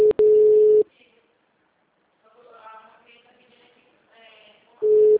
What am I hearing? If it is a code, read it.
Ringback tone (Europe, 425 Hz), with speech.
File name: out- -2011- - - .1370